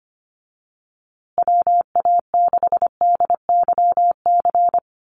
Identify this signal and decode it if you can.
Morse code, heard as a keyed tone, about 25 wpm, 700 Hz. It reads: WA6DYC
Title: USA Call Signs Repeated 3x 25wpm